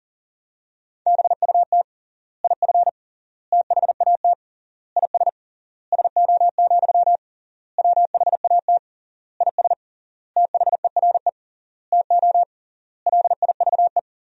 Morse code, heard as a keyed tone, about 40 wpm, 700 Hz. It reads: BUT IF THAT IS SO, WHAT IS THERE TO LIVE
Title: 3x Word-Spacing - Sentences from Top 200 Words - 40wpm